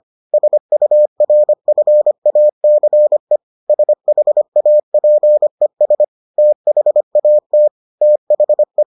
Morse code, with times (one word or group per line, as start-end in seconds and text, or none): 0.3-3.4 SURFACE
3.7-6.0 SHAPES
6.4-7.7 THAT
8.0-8.8 THE